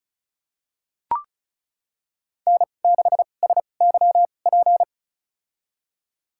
Morse code, heard as a keyed tone, about 36 wpm, spaced as for 25 wpm, 700 Hz. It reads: N6SYP
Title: USA Call Signs - 5 Characters - 2x Character Spacing - 35wpm